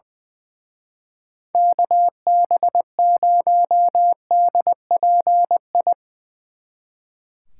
Morse code, heard as a keyed tone, about 22 wpm, 700 Hz. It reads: KB0DPI
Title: Speed Racing - USA 2x3 Call Signs 20wpm